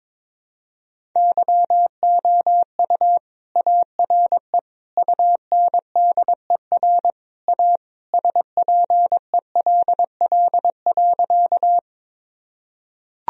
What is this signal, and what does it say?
Morse code, heard as a keyed tone, about 21 wpm, 700 Hz. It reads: YOU ARE UNDER A SPELL.
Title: Sentences from Top 1000 Words - Review Part 1 - 22wpm